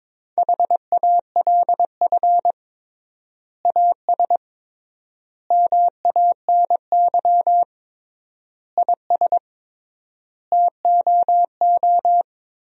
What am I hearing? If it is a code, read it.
Morse code, heard as a keyed tone, about 22 wpm, 700 Hz. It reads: HALF AS MANY IS TOO
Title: 3x Word-Spacing - Sentences from Top 400 Words - 22wpm